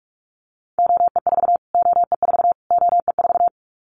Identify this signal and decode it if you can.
Morse code, heard as a keyed tone, about 44 wpm, 700 Hz. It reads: OE4 OE4 OE4
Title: Callsign Prefixes for Top 25 DXCC - Repeated 3x - 45wpm